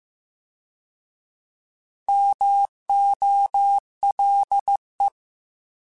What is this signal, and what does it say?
Morse code, heard as a keyed tone, about 15 wpm, 800 Hz. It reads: MOLE
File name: Alphabetical / Alphabetical mp3s / Ms @13